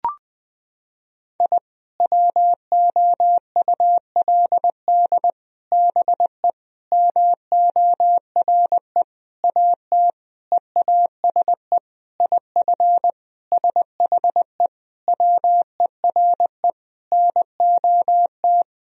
Morse code, heard as a keyed tone, about 20 wpm, 700 Hz. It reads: I WOULD BE MORE AT EASE IF SHE WERE NOT
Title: Sentences from Top 1000 Words - Review Part 1 - 20wpm